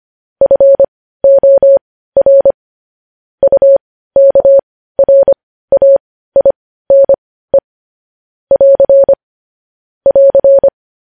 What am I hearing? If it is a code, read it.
Morse code, heard as a keyed tone, about 25 wpm, spaced as for 15 wpm, 550 Hz. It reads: FOR UKRAINE <AR> <AR>
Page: News Headlines in Morse Code at 15 WPM